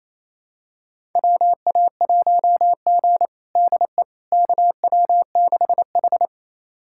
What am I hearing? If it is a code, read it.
Morse code, heard as a keyed tone, about 28 wpm, 700 Hz. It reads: WA1G DE KW6H
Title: USA Callsign DE Callsign - 4 Characters - 28wpm